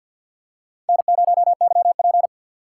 Morse code, heard as a keyed tone, about 50 wpm, 700 Hz. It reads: N0YP
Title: USA Call Signs - 4x Character Spacing - 50wpm